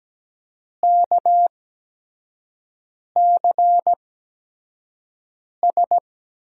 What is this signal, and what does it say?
Morse code, heard as a keyed tone, about 17 wpm, spaced as for 5 wpm, 700 Hz. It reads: KCS
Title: USA Call Signs - 5 Characters - 8x Character Spacing - 17wpm